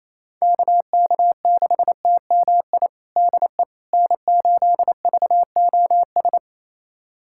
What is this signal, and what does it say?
Morse code, heard as a keyed tone, about 28 wpm, 700 Hz. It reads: KK6TMI DE N8VOS
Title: USA Callsign DE Callsign 28wpm